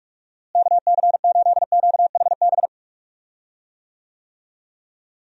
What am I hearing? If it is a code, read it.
Morse code, heard as a keyed tone, about 46 wpm, 700 Hz. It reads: KC8QHB
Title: USA Call Signs - 6 Characters - Repeated 3x - 45wpm